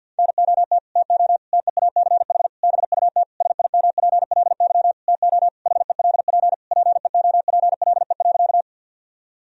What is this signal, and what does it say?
Morse code, heard as a keyed tone, about 50 wpm, 700 Hz. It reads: NOT TO TEACH BUT SIMPLY TO HELP PEOPLE.
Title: Sentences from Top 1000 Words - Review Part 4 - 50wpm